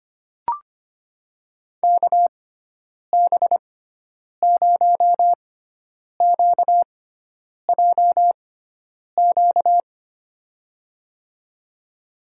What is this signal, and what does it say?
Morse code, heard as a keyed tone, about 25 wpm, spaced as for 9 wpm, 700 Hz. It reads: KB0QJQ
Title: USA Call Signs - 6 Characters - 6x Character Spacing - 25wpm